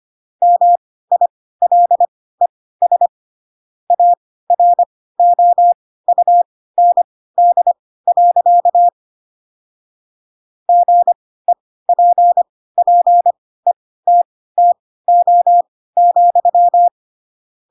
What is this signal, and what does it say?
Morse code, heard as a keyed tone, about 25 wpm, spaced as for 16 wpm, 700 Hz. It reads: MILES AROUND. GEPPETTO,
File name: Pinocchio / Pinocchio ch7 10 25 ik1hge